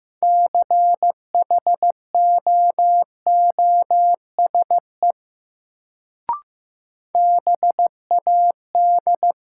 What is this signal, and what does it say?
Morse code, heard as a keyed tone, about 15 wpm, 700 Hz. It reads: CHOOSE BAD
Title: Sentences from Top 700 Words - New Word Drill - 15wpm